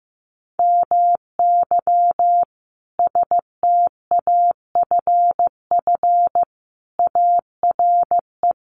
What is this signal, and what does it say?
Morse code, heard as a keyed tone, about 15 wpm, 700 Hz. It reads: MY STAFF ARE
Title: Sentences from Top 1200 Words 15wpm